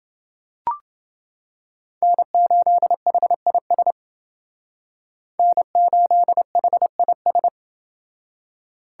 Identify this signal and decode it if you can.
Morse code, heard as a keyed tone, about 28 wpm, 700 Hz. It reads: N8HIS N8HIS